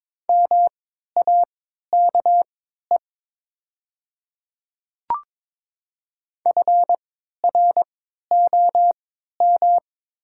Morse code, heard as a keyed tone, about 22 wpm, spaced as for 12 wpm, 700 Hz. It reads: MAKE FROM
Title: 3x Character Spacing - Top 100 Words - 22wpm